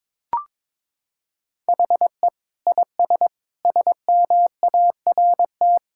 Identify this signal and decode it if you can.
Morse code, heard as a keyed tone, about 22 wpm, 700 Hz. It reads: HE IS SMART
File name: practice / 22wpm / Sentences with 4-Word Phrases - 22wpm